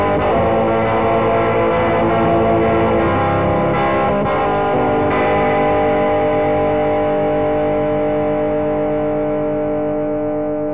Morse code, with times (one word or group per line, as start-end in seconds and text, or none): none